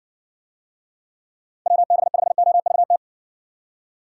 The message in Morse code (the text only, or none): WB5OVT